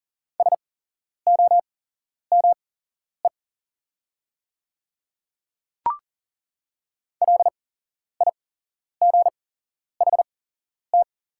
Morse code, heard as a keyed tone, about 40 wpm, spaced as for 11 wpm, 700 Hz. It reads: SOME LIGHT
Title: 8x Character Spacing - Top 500 Words - 40wpm